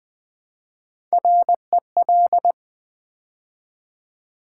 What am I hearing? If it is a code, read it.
Morse code, heard as a keyed tone, about 20 wpm, 700 Hz. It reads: REL
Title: Build A Word - 3 Letters At A Time - 8-Letter Words - 20wpm